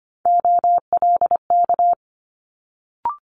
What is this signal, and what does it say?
Morse code, heard as a keyed tone, about 25 wpm, 700 Hz. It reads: OLK
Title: Multi-letter Phonemes Quick Fire 25wpm